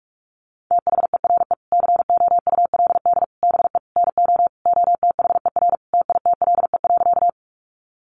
Morse code, heard as a keyed tone, about 44 wpm, 700 Hz. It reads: THERE COULD BE NO OTHER TITLE.
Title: Sentences from Top 1200 Words 45wpm